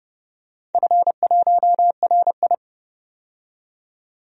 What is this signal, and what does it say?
Morse code, heard as a keyed tone, about 30 wpm, 700 Hz. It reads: F1RI
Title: DXCC-100 Call Signs - 4 Characters - 30wpm